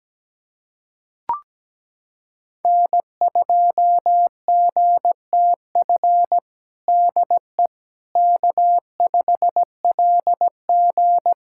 Morse code, heard as a keyed tone, about 17 wpm, 700 Hz. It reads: N2GTF DE K5LG
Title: Practice QSO - Exchange 4 - Ending - 17wpm